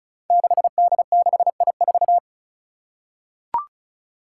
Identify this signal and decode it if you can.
Morse code, heard as a keyed tone, about 36 wpm, 700 Hz. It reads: 6D6I4
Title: Five Challenging Letter-Number Combinations 35wpm